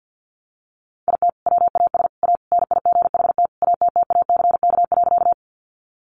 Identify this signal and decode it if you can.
Morse code, heard as a keyed tone, about 50 wpm, 700 Hz. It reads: IT WAS A NIGHT ATTACK.
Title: Sentences from Top 1500 Words 50wpm